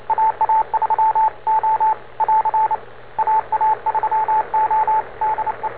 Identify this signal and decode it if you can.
Morse code, heard as a keyed tone, about 29 wpm, 870 Hz, with radio noise. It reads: AA3O <AR> AA3O DE
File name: n4st / rs10 robot aa3o